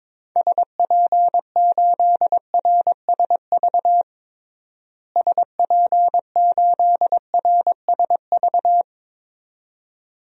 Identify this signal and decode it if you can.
Morse code, heard as a keyed tone, about 22 wpm, 700 Hz. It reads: SP8RSV SP8RSV
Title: DXCC-100 Call Signs - 6 Characters - Repeated 2x - 22wpm